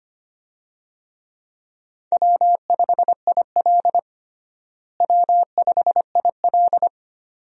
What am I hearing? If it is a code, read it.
Morse code, heard as a keyed tone, about 25 wpm, 700 Hz. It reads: W5IL W5IL
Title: USA Call Signs - 4 Characters - Repeated 2x - 25wpm